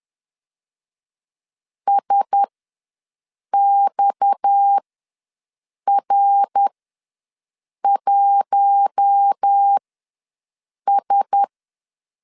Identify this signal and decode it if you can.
Morse code, heard as a keyed tone, about 10 wpm, spaced as for 6 wpm, 800 Hz. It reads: SXR1S
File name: LEZIONI / 6 PKX0 / 6 PKX0 ws 10wpm 4220